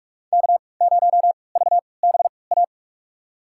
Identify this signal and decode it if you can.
Morse code, heard as a keyed tone, about 42 wpm, spaced as for 25 wpm, 700 Hz. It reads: K0VBA
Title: USA Call Signs - 5 Characters - 3x Character Spacing - 45wpm